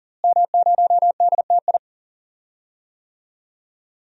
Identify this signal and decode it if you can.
Morse code, heard as a keyed tone, about 37 wpm, 700 Hz. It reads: M0DTI
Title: DXCC-100 Call Signs - 5 Characters - Repeated 3x - 40wpm